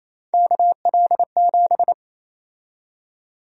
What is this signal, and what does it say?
Morse code, heard as a keyed tone, about 28 wpm, 700 Hz. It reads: KL7
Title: USA Call Sign Prefixes - 28wpm